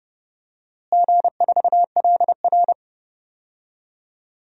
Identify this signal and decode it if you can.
Morse code, heard as a keyed tone, about 30 wpm, 700 Hz. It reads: G4LR
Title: DXCC-100 Call Signs - 4 Characters - Repeated 2x - 30wpm